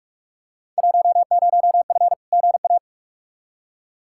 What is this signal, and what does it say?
Morse code, heard as a keyed tone, about 46 wpm, 700 Hz. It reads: 10F GA
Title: Field Day Exchanges Encore - 45wpm